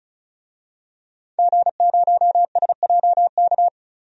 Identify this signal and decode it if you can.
Morse code, heard as a keyed tone, about 34 wpm, 700 Hz. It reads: G0SJK
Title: DXCC-100 Call Signs - 5 Characters - 35wpm